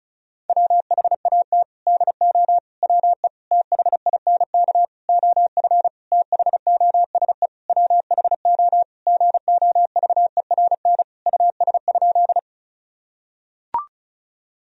Morse code, heard as a keyed tone, about 36 wpm, 700 Hz. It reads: WHAT DO WE THINK OF THOSE WHO GOVERN US?
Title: Sentences from Top 1000 Words - Review Part 2 - 35wpm